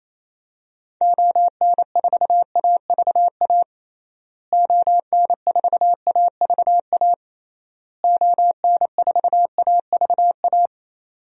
Morse code, heard as a keyed tone, about 28 wpm, 700 Hz. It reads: ON4AVA ON4AVA ON4AVA
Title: DXCC-100 Call Signs - 6 Characters - Repeated 3x - 28wpm